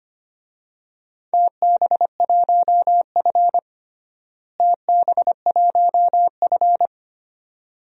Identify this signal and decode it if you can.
Morse code, heard as a keyed tone, about 25 wpm, 700 Hz. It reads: TB1F TB1F
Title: DXCC-100 Call Signs - 4 Characters - Repeated 3x - 25wpm